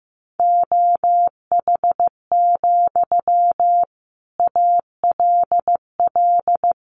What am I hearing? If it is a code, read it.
Morse code, heard as a keyed tone, about 15 wpm, 700 Hz. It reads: OH, ALL
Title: Sentences from Top 500 Words 15wpm